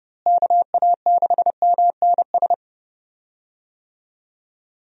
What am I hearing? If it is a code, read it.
Morse code, heard as a keyed tone, about 32 wpm, 700 Hz. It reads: KA6MNS
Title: USA Call Signs - 6 Characters - Repeated 3x - 30wpm